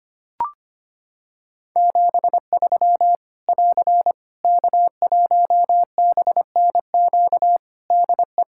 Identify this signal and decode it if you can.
Morse code, heard as a keyed tone, about 25 wpm, 700 Hz. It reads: 73 <AR> K1BNQ DE
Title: Practice QSO - Exchange 4 - Ending - 25wpm